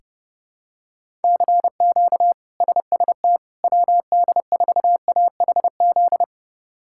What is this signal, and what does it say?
Morse code, heard as a keyed tone, about 30 wpm, 700 Hz. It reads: CQ SST WD4AHZ
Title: K1USN SST QSO Practice - 30wpm